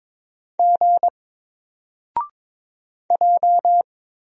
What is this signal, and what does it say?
Morse code, heard as a keyed tone, about 22 wpm, 700 Hz. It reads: G J
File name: Single-Letter-Number-22wpm